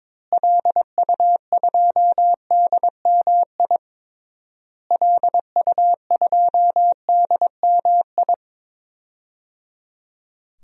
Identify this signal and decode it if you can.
Morse code, heard as a keyed tone, about 22 wpm, 700 Hz. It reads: LU2DMI LU2DMI
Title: DXCC-100 Call Signs - 6 Characters - Repeated 3x - 22wpm